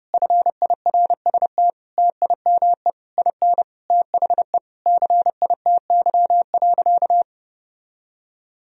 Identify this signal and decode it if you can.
Morse code, heard as a keyed tone, about 32 wpm, 700 Hz. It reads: FIRST TIME IN THE CITY.
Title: Sentences from Top 1000 Words - Review Part 4 - 30wpm